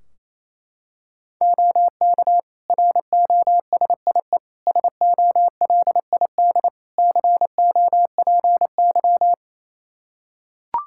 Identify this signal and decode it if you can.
Morse code, heard as a keyed tone, about 27 wpm, 700 Hz. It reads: OK ROSIE SOLID COPY
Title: Practice QSO - Exchange 3 - More Info - 28wpm